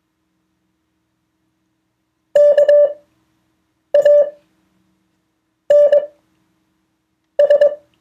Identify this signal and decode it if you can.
Morse code, heard as a keyed tone, about 21 wpm, spaced as for 6 wpm, 580 Hz. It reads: KANS